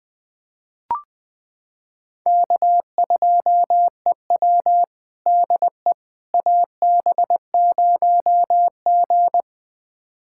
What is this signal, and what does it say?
Morse code, heard as a keyed tone, about 20 wpm, 700 Hz. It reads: K2EW DE AB0G
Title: USA Callsign DE Callsign - 4 Characters - 20wpm